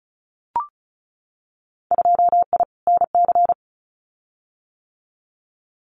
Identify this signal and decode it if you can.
Morse code, heard as a keyed tone, about 34 wpm, 700 Hz. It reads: 2I NC
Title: Winter Field Day Exchanges 35wpm